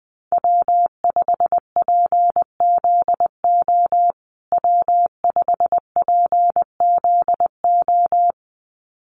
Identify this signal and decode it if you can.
Morse code, heard as a keyed tone, about 20 wpm, 700 Hz. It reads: W5PZO W5PZO
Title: USA Call Signs Repeated 2x 20wpm